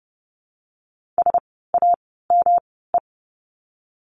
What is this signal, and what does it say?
Morse code, heard as a keyed tone, about 30 wpm, spaced as for 17 wpm, 700 Hz. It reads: SAME
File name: course / 30:10wpm / 237 Lesson 38f - Letter q - Words 30wpm